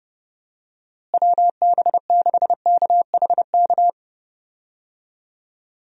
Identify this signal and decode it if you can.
Morse code, heard as a keyed tone, about 29 wpm, 700 Hz. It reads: WB6KHK
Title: USA Call Signs - 6 Characters - Repeated 3x - 30wpm